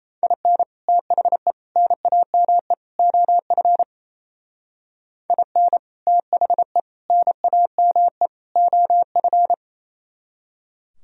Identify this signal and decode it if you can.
Morse code, heard as a keyed tone, about 33 wpm, 700 Hz. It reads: IN THE NAME OF IN THE NAME OF
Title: Speed-Racing - Sets of 4 Words 28wpm